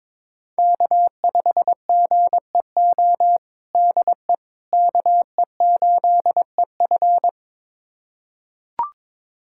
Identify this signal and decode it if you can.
Morse code, heard as a keyed tone, about 22 wpm, 700 Hz. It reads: K5GEO DE KE8EF
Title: USA Callsign DE Callsign - 5 Characters - 22wpm